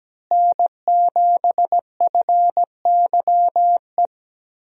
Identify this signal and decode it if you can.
Morse code, heard as a keyed tone, about 17 wpm, 700 Hz. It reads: N7FYE
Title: USA Call Signs - 5 Characters - Repeated 3x - 17wpm